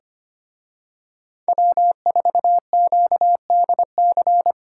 Morse code, heard as a keyed tone, about 29 wpm, 700 Hz. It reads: W4QDC
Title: Speed Racing - USA 1x3 Call Signs 25wpm